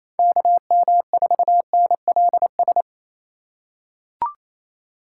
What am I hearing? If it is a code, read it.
Morse code, heard as a keyed tone, about 27 wpm, 700 Hz. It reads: KM4NLS
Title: USA Call Signs - 6 Characters - 6x Character Spacing - 28wpm